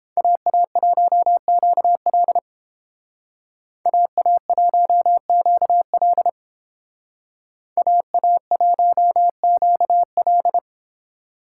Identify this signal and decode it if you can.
Morse code, heard as a keyed tone, about 30 wpm, 700 Hz. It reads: AA1QL AA1QL AA1QL